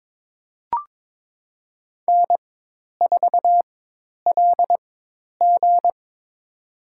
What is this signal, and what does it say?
Morse code, heard as a keyed tone, about 22 wpm, spaced as for 10 wpm, 700 Hz. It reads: N4LG
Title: USA Call Signs - 4 Characters - 4x Character Spacing - 22wpm